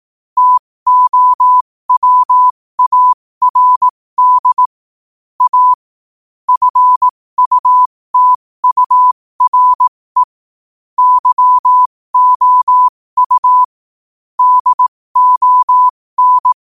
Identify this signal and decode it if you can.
Morse code, heard as a keyed tone, about 18 wpm, spaced as for 15 wpm, 1000 Hz. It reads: TOWARD A FUTURE YOU DON
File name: qotdfiles / 15wpm